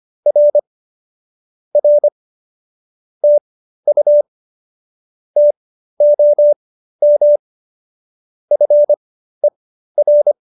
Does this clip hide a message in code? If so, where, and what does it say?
Yes: Morse code from 0.3 to 10.3 s, heard as a keyed tone, about 25 wpm, spaced as for 13 wpm, 570 Hz. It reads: R R TU TOM FER